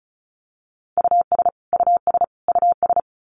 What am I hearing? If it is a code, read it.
Morse code, heard as a keyed tone, about 34 wpm, 700 Hz. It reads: US US US
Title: Top 200 Words Repeated 35wpm